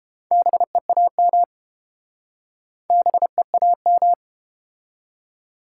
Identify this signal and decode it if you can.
Morse code, heard as a keyed tone, about 29 wpm, 700 Hz. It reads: BEAM BEAM